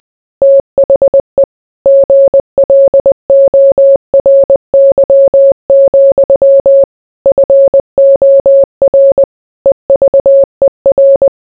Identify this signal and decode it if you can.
Morse code, heard as a keyed tone, about 20 wpm, 550 Hz. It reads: THE GLORY, FOR EVER